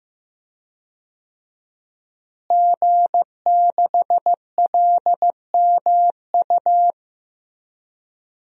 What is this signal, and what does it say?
Morse code, heard as a keyed tone, about 15 wpm, 700 Hz. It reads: G6LMU
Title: DXCC-100 Call Signs - 5 Characters - Repeated 3x - 15wpm